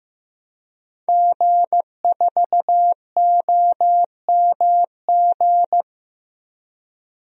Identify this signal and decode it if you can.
Morse code, heard as a keyed tone, about 15 wpm, 700 Hz. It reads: G4OMG
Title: DXCC-100 Call Signs - 5 Characters - Repeated 3x - 15wpm